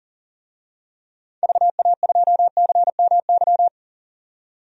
Morse code, heard as a keyed tone, about 38 wpm, 700 Hz. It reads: VA2CMY
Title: DXCC-100 Call Signs - 6 Characters - 40wpm